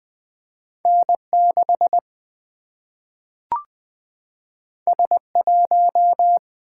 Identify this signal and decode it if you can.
Morse code, heard as a keyed tone, about 20 wpm, 700 Hz. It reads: N6 S1
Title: Single-Letter Single-Number 20wpm